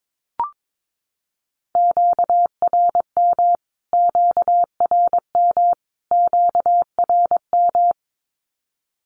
Morse code, heard as a keyed tone, about 22 wpm, 700 Hz. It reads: QRM QRM QRM
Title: Q-Codes Repeated 3x 22wpm